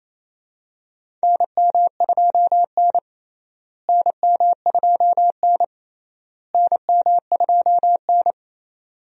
Morse code, heard as a keyed tone, about 28 wpm, 700 Hz. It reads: NM2N NM2N NM2N
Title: USA Call Signs - 4 Characters - Repeated 3x - 28wpm